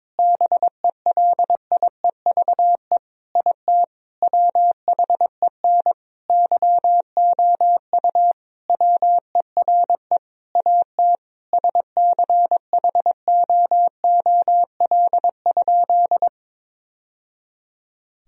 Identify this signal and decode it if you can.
Morse code, heard as a keyed tone, about 22 wpm, 700 Hz. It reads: BELIEVE IT WHEN YOU WERE AT SCHOOL?
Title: Sentences from Top 1000 Words - Review Part 1 - 22wpm